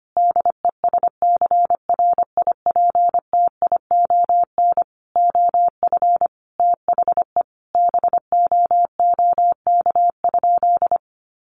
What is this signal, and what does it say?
Morse code, heard as a keyed tone, about 25 wpm, 700 Hz. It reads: DESCRIPTION OF THE BOOK?
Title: Sentences from Top 1300 Words 25wpm